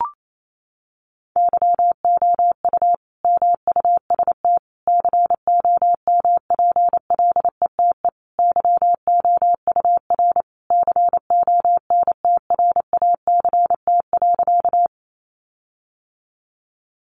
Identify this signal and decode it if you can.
Morse code, heard as a keyed tone, about 28 wpm, 700 Hz. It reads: YOU MUST COMPLETE YOUR CONTRACT.
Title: Sentences from Top 1400 Words 28wpm